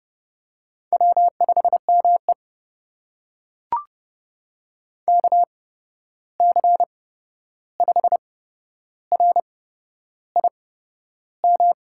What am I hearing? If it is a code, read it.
Morse code, heard as a keyed tone, about 30 wpm, spaced as for 11 wpm, 700 Hz. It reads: W5ME KC5RIM
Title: USA Call Signs - 8x Character Spacing - 30wpm